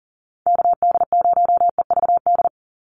Morse code, heard as a keyed tone, about 40 wpm, 700 Hz. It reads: KD0EVD
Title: USA Call Signs 40wpm